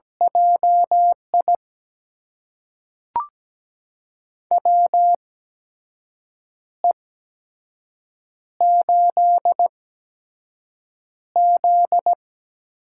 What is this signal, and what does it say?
Morse code, heard as a keyed tone, about 17 wpm, spaced as for 6 wpm, 700 Hz. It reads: JI WE8Z
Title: USA Call Signs - 4 Characters - 8x Character Spacing - 17wpm